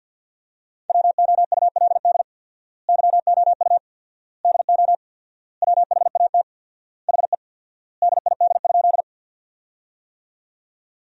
Morse code, heard as a keyed tone, about 50 wpm, 700 Hz. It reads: WOULD YOU DO WHAT HE DID?
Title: 4x Word-Spacing - Sentences from Top 100 Words - 50wpm